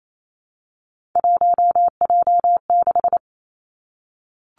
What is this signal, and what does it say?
Morse code, heard as a keyed tone, about 28 wpm, 700 Hz. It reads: 1J6
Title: Three Challenging Letter-Number Combinations 28wpm